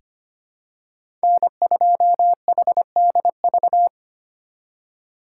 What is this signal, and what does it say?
Morse code, heard as a keyed tone, about 25 wpm, 700 Hz. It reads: N2HDV
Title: USA Call Signs - 5 Characters - 4x Character Spacing - 25wpm